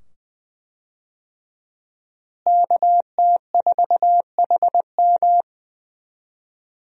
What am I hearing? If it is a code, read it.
Morse code, heard as a keyed tone, about 20 wpm, 700 Hz. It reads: KT4HM
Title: USA Call Signs - 5 Characters - Repeated 2x - 20wpm